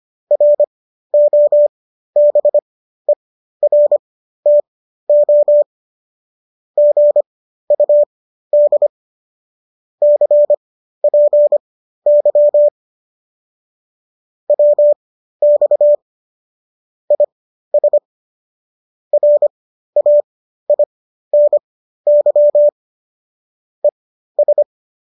Morse code, heard as a keyed tone, about 25 wpm, spaced as for 13 wpm, 590 Hz. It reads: ROBERTO GUD CPY WX IS RAINY ES